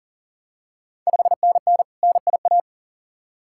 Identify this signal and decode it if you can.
Morse code, heard as a keyed tone, about 42 wpm, 690 Hz. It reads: 5NN NIA
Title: Ontario QSO Party Exchange 40wpm